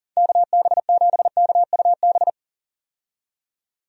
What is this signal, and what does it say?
Morse code, heard as a keyed tone, about 40 wpm, 700 Hz. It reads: KB7KUB
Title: USA Call Signs - 6 Characters - 40wpm